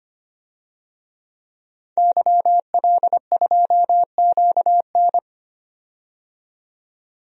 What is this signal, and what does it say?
Morse code, heard as a keyed tone, about 25 wpm, 700 Hz. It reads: YL2QN
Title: DXCC-100 Call Signs - 5 Characters - 25wpm